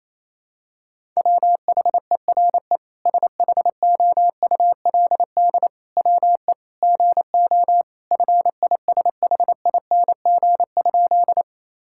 Morse code, heard as a keyed tone, about 28 wpm, 700 Hz. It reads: WHERE SHOULD WE GO FISHING?
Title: Sentences from Top 1000 Words - Review Part 5 - 28wpm